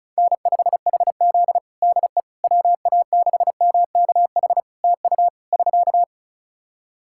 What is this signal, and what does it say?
Morse code, heard as a keyed tone, about 36 wpm, 700 Hz. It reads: N5HZ DE WA6MKH TU <SK>
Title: Practice QSO - Exchange 4 - Ending - 35wpm